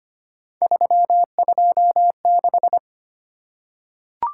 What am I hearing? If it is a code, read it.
Morse code, heard as a keyed tone, about 25 wpm, 700 Hz. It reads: 326
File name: Three Letter-Number Combinations - 25wpm